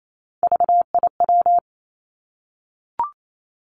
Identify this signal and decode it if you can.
Morse code, heard as a keyed tone, about 28 wpm, 700 Hz. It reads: VIW
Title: Three Challenging Letter-Number Combinations 28wpm